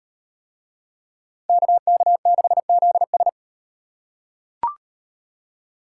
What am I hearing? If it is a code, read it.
Morse code, heard as a keyed tone, about 38 wpm, 700 Hz. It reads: KK6ZS